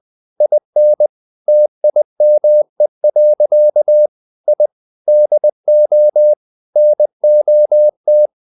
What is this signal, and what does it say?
Morse code, heard as a keyed tone, about 20 wpm, 600 Hz. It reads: IN TIME. I DO NOT
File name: the old man and the sea / the old man and the sea part 1 20wpm